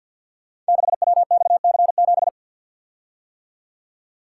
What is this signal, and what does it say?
Morse code, heard as a keyed tone, about 49 wpm, 700 Hz. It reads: 6WXC7
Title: Code Groups - 5 Letter:Numbers - 50wpm